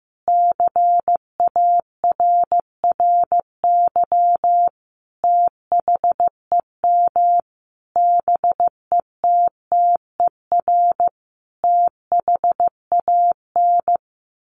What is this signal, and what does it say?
Morse code, heard as a keyed tone, about 15 wpm, 700 Hz. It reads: CARRY THEM BETTER THAN
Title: Sentences from Top 1200 Words 15wpm